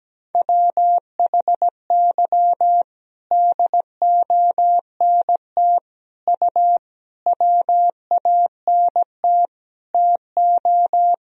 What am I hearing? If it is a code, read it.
Morse code, heard as a keyed tone, about 17 wpm, 700 Hz. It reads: WHY DONT U WANT TO
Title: Sentences from Top 300 Words and Abbreviations 17wpm